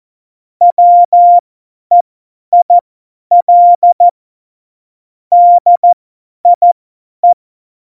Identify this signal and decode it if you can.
Morse code, heard as a keyed tone, about 14 wpm, spaced as for 10 wpm, 700 Hz. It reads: WEIL DIE
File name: Morsekurs / Witze / Witz 25 Juli 17, Tempo 50 F70